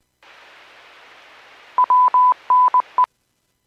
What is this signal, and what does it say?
Morse code, heard as a keyed tone, about 20 wpm, 1000 Hz. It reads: WNE
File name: Businesses / 00235 13-51 Businesses 451.25 WPS